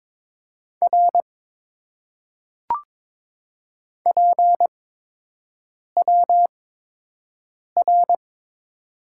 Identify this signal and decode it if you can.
Morse code, heard as a keyed tone, about 22 wpm, spaced as for 6 wpm, 700 Hz. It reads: R PWR
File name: Top 90 QSO Elements - 8x Character Spacing - 22wpm